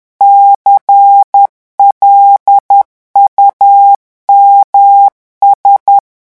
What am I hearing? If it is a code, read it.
Morse code, heard as a keyed tone, about 10 wpm, 800 Hz. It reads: CLUMS
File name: morse mp3 / revision5-vitesse 10 mots